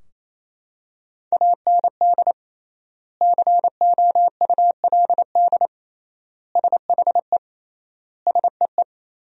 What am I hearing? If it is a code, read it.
Morse code, heard as a keyed tone, about 28 wpm, 700 Hz. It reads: AND COULD SHE SEE
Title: 3x Word-Spacing - Sentences from Top 100 Words - 28wpm